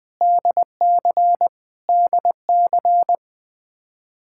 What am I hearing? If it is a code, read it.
Morse code, heard as a keyed tone, about 20 wpm, 700 Hz. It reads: DC DC
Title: Practice QSO - Exchange 1 - Beginning - 20wpm